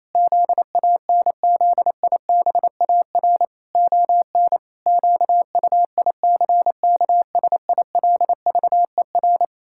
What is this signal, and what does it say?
Morse code, heard as a keyed tone, about 27 wpm, 700 Hz. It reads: ZANZIBAR ON QUICKSILVER
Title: Pangram Challenge v2 - 28wpm